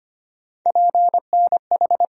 Morse code, heard as a keyed tone, about 25 wpm, 700 Hz. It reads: PNH
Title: USA Call Sign Postfixes - 3 Letters 25wpm